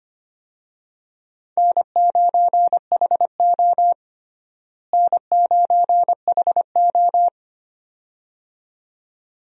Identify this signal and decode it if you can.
Morse code, heard as a keyed tone, about 25 wpm, 700 Hz. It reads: N9HO N9HO
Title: USA Call Signs - 4 Characters - Repeated 2x - 25wpm